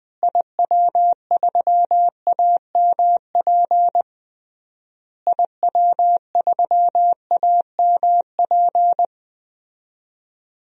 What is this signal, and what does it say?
Morse code, heard as a keyed tone, about 21 wpm, 700 Hz. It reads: IW3AMP IW3AMP
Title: DXCC-100 Call Signs - 6 Characters - Repeated 3x - 20wpm